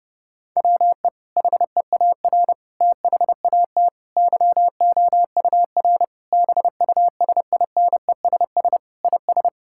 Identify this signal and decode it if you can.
Morse code, heard as a keyed tone, about 32 wpm, 700 Hz. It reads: WE HEAR THAT YOUR BUSINESS IS
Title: Sentences from Top 1000 Words - Review Part 5 - 30wpm